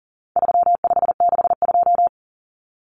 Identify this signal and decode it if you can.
Morse code, heard as a keyed tone, about 40 wpm, 700 Hz. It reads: 3562
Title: Numbers - 4-Digit 40wpm